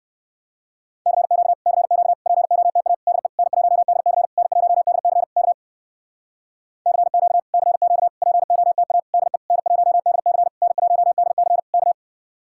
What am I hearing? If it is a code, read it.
Morse code, heard as a keyed tone, about 58 wpm, 700 Hz. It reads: CQ CQ POTA DE N1DC N1DC K CQ CQ POTA DE N1DC N1DC K